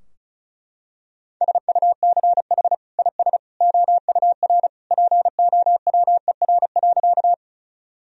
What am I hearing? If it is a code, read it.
Morse code, heard as a keyed tone, about 34 wpm, 700 Hz. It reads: SUCH IS OUR POWER.
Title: Sentences from Top 1000 Words - Review Part 2 - 35wpm